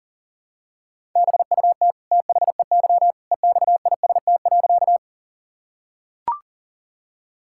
Morse code, heard as a keyed tone, about 37 wpm, 700 Hz. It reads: BUT THEY EXIST.
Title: Sentences from Top 1000 Words - Review Part 5 - 40wpm